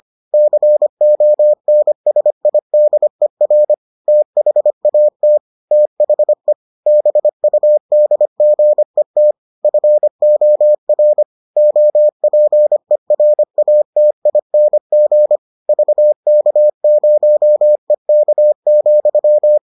Morse code, heard as a keyed tone, about 25 wpm, 600 Hz. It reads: CONSIDER THAT THE BUDGET FOR OPERATING VK0EK,